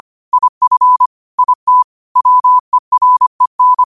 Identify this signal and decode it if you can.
Morse code, heard as a keyed tone, about 25 wpm, 1000 Hz. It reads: IF IT WEREN
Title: 25 WPM morse code quotes for Thu, 29 Feb 2024